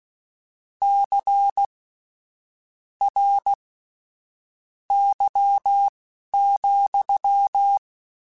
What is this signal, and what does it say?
Morse code, heard as a keyed tone, about 16 wpm, spaced as for 7 wpm, 800 Hz. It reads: CRY,
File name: Misc / Misc-mp3s / WordPairs sp @10